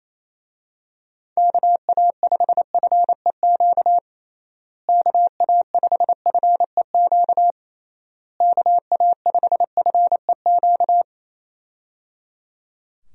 Morse code, heard as a keyed tone, about 28 wpm, 700 Hz. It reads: KA5FEQ KA5FEQ KA5FEQ
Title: USA Call Signs - 6 Characters - Repeated 3x - 28wpm